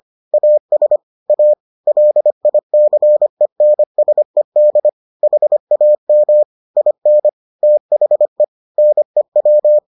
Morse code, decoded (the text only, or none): AS A LICENSED HAM IN THE NEW